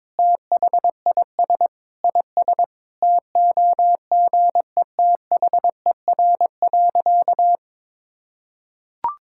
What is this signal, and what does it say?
Morse code, heard as a keyed tone, about 22 wpm, 700 Hz. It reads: THIS IS TOGETHER.
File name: practice / 22wpm / Sentences with 4-Word Phrases - 22wpm